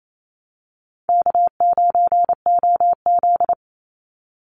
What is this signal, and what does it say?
Morse code, heard as a keyed tone, about 28 wpm, 700 Hz. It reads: K9OZ
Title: NA SOTA Chaser Callsigns 28wpm